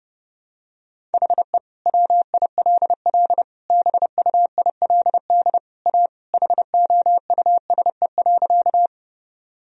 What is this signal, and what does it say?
Morse code, heard as a keyed tone, about 30 wpm, 700 Hz. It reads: HE WILL BUILD A HOUSE.
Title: Sentences from Top 1000 Words - Review Part 1 - 30wpm